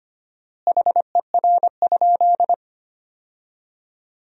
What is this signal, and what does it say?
Morse code, heard as a keyed tone, about 25 wpm, 700 Hz. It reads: HER?
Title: 4x Word-Spacing - Sentences from Top 100 Words - 25wpm